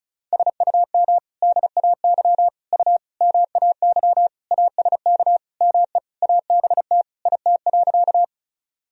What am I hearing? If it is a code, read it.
Morse code, heard as a keyed tone, about 34 wpm, 700 Hz. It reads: SUM DAY U MAY ASK ME ABT IT.
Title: Sentences from Top 200 Words and Abbreviations 35wpm